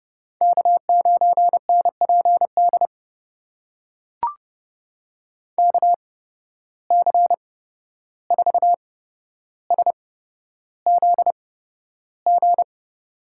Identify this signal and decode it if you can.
Morse code, heard as a keyed tone, about 30 wpm, spaced as for 12 wpm, 700 Hz. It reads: K9NPD KC4SZG
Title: USA Call Signs - 8x Character Spacing - 30wpm